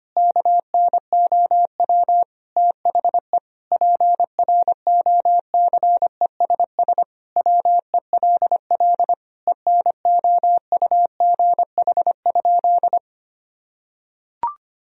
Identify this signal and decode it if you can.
Morse code, heard as a keyed tone, about 25 wpm, 700 Hz. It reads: KNOW THE PROCESS WELL ENOUGH?
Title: Sentences from Top 1000 Words - Review Part 3 - 25wpm